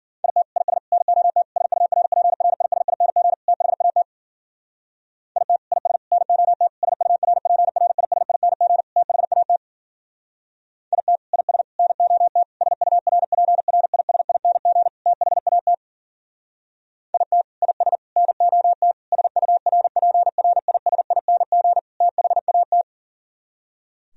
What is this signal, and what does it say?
Morse code, heard as a keyed tone, about 51 wpm, 700 Hz. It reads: IT IS NOT SURPRISING THAT IT IS NOT SURPRISING THAT IT IS NOT SURPRISING THAT IT IS NOT SURPRISING THAT